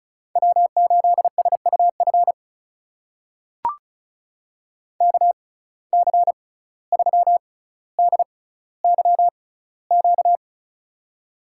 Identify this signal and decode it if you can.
Morse code, heard as a keyed tone, about 36 wpm, spaced as for 17 wpm, 700 Hz. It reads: W8SUF KC3DYQ
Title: USA Call Signs - 6x Character Spacing - 35wpm